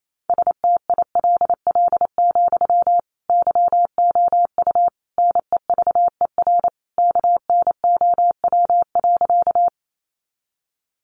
Morse code, heard as a keyed tone, about 28 wpm, 700 Hz. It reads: STILL, YOU NEVER KNOW.
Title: Sentences from Top 300 Words 28wpm